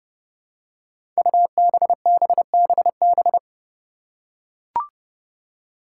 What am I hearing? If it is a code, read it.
Morse code, heard as a keyed tone, about 30 wpm, 700 Hz. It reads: UBBBB
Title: Five Challenging Letter-Number Combinations 30wpm